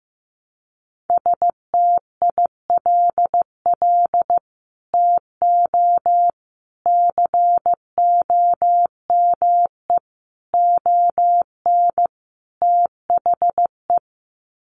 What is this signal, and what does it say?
Morse code, heard as a keyed tone, about 15 wpm, 700 Hz. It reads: STILL TO COME ON THE
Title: Sets of 5 Words Encore2 15wpm